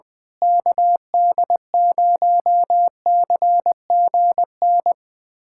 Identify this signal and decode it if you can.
Morse code, heard as a keyed tone, about 20 wpm, 700 Hz. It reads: KD0CGN
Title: USA Callsign DE Callsign - 6 Characters - 20wpm